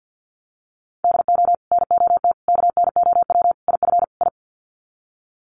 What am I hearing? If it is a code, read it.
Morse code, heard as a keyed tone, about 50 wpm, 700 Hz. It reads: DO NOT KNOW IF I
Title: Sets of 5 Words 50wpm